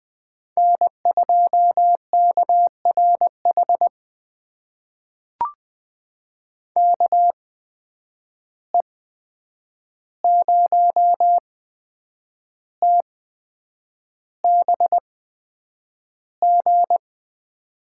Morse code, heard as a keyed tone, about 20 wpm, spaced as for 8 wpm, 700 Hz. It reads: N2KRH KE0TBG
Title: USA Call Signs - 8x Character Spacing - 20wpm